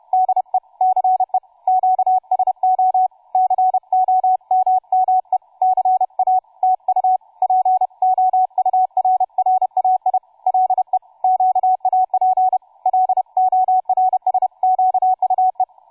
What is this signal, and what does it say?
Morse code, heard as a keyed tone, about 31 wpm, 750 Hz, with radio noise. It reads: DE CE QSO COMME CA TU POURRAI LE QAP LORSQUE
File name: cw forum / tu2 f5ldy 18mhz 16h46